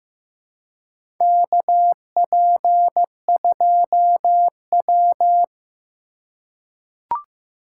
Morse code, heard as a keyed tone, about 15 wpm, 700 Hz. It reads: KP2W
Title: USA Call Signs - 4 Characters - 3x Character Spacing - 15wpm